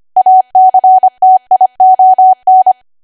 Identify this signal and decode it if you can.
Morse code, heard as a keyed tone, about 25 wpm, 750 Hz. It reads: ACTION